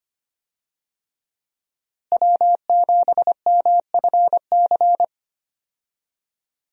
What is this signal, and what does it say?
Morse code, heard as a keyed tone, about 25 wpm, 700 Hz. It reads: W7MFC
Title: USA Call Signs - 5 Characters - Repeated 3x - 25wpm